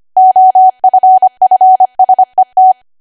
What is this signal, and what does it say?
Morse code, heard as a keyed tone, about 25 wpm, 750 Hz. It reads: OFFSET
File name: cwa / W1AW-Level3 / 25WPM